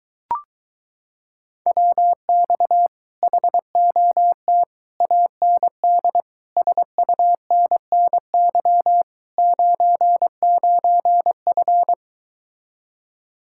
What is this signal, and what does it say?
Morse code, heard as a keyed tone, about 23 wpm, 700 Hz. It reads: WX HOT AND SUNNY 99F